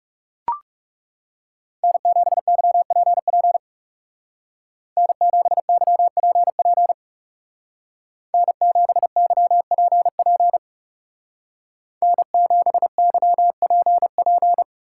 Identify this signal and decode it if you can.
Morse code, heard as a keyed tone, about 34 wpm, 700 Hz. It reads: N7YPP N7YPP N7YPP N7YPP
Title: Speed Racing - USA 1x3 Call Signs 30wpm